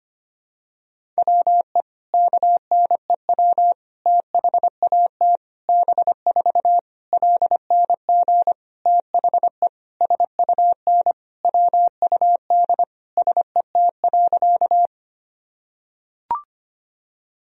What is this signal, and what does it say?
Morse code, heard as a keyed tone, about 25 wpm, 700 Hz. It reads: WE KNEW THAT B4 LNG THE SUN WUD SET.
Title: Sentences from Top 400 Words and Abbreviations 25wpm